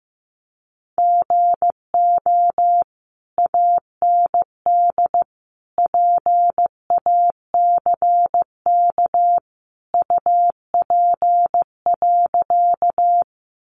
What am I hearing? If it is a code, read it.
Morse code, heard as a keyed tone, about 15 wpm, 700 Hz. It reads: GO AND PACK UP.
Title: Sentences from Top 1300 Words 15wpm